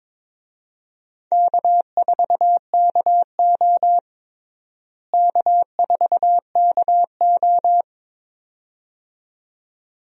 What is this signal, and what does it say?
Morse code, heard as a keyed tone, about 22 wpm, 700 Hz. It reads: K4KO K4KO
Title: USA Call Signs - 4 Characters - Repeated 2x - 22wpm